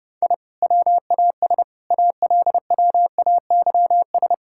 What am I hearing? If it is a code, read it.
Morse code, heard as a keyed tone, about 30 wpm, 700 Hz. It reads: I WAS ALWAYS
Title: Sentences from Top 1000 Words - Review Part 6 - 30wpm